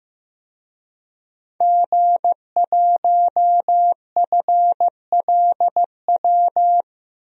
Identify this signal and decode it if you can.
Morse code, heard as a keyed tone, about 15 wpm, 700 Hz. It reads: G1FLW
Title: DXCC-100 Call Signs - 5 Characters - Repeated 3x - 15wpm